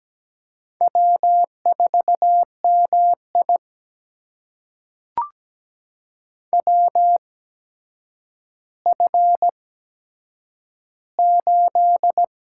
Morse code, heard as a keyed tone, about 17 wpm, spaced as for 8 wpm, 700 Hz. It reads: W4MI WF8
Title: USA Call Signs - 8x Character Spacing - 17wpm